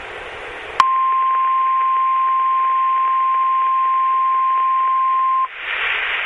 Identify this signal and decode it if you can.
Emergency Alert System (SAME) tones (the 1050 Hz attention tone). It is heard from 0.8 to 5.5 s.
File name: Washtenaw County SKYWARN N8DUY and KT8TD Repeaters